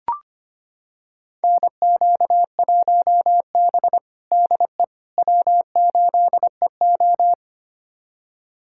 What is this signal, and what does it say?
Morse code, heard as a keyed tone, about 25 wpm, 700 Hz. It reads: NQ1B DE W8EO
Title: USA Callsign DE Callsign - 4 Characters - 25wpm